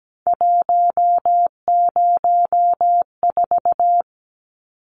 Morse code, heard as a keyed tone, about 17 wpm, 700 Hz. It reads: 104
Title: Numbers - 3-Digit 17wpm